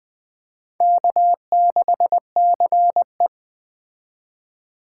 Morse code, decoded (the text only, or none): K6CE